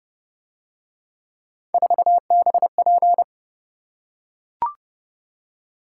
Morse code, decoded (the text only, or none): 4BP